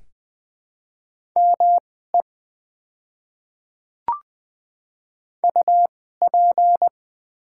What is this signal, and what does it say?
Morse code, heard as a keyed tone, about 20 wpm, spaced as for 15 wpm, 700 Hz. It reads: ME UP
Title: 2x Character Spacing - Top 200 Words - 20wpm